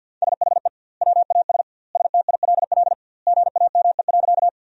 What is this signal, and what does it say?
Morse code, heard as a keyed tone, about 52 wpm, 700 Hz. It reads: SHE WAS STILL GAME.
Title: 2x Word-Spacing - Sentences from Top 500 Words - 50wpm